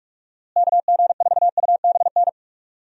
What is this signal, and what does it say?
Morse code, heard as a keyed tone, about 44 wpm, 700 Hz. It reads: KG4UBN
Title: USA Call Signs - 6 Characters - Repeated 3x - 45wpm